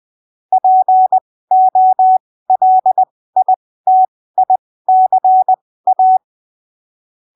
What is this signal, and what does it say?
Morse code, heard as a keyed tone, about 20 wpm, spaced as for 15 wpm, 750 Hz. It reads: POLITICA
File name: arquivos / texto / texto 24 20ppm